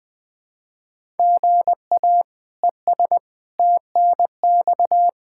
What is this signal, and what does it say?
Morse code, heard as a keyed tone, about 21 wpm, 700 Hz. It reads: GA ES TNX
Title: K1USN SST QSO Practice 20wpm